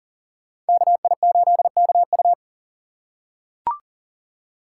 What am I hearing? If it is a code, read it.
Morse code, heard as a keyed tone, about 38 wpm, 700 Hz. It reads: KI8KU
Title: USA Call Signs - 5 Characters - 3x Character Spacing - 40wpm